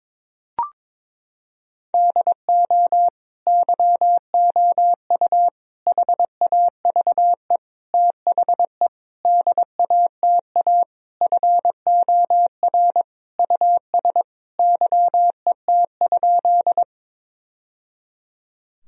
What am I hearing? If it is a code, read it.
Morse code, heard as a keyed tone, about 22 wpm, 700 Hz. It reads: DO YOU HAVE THE DATA FOR US YET?
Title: Sentences from Top 1000 Words - Review Part 3 - 22wpm